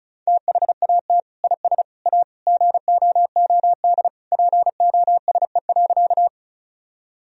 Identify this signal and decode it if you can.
Morse code, heard as a keyed tone, about 36 wpm, 700 Hz. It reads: THAT IS A GOOD POSE.
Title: Sentences from Top 1000 Words - Review Part 2 - 35wpm